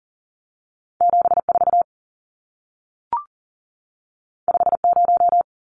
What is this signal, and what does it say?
Morse code, heard as a keyed tone, about 40 wpm, 700 Hz. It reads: 74 50
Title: Numbers - 2-Digit 40wpm